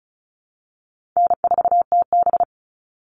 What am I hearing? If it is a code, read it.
Morse code, heard as a keyed tone, about 34 wpm, 700 Hz. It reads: N4TB
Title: Call Signs Repeated 2x - 35wpm